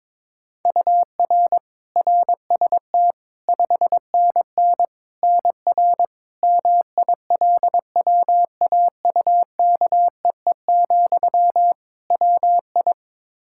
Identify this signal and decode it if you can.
Morse code, heard as a keyed tone, about 21 wpm, 700 Hz. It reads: UR RST 5NN NR MILWAUKEE, WI
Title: Practice QSO - Exchange 1 - Beginning - 22wpm